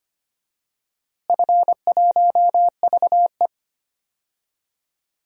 Morse code, heard as a keyed tone, about 25 wpm, 700 Hz. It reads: F1VE
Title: DXCC-100 Call Signs - 4 Characters - Repeated 3x - 25wpm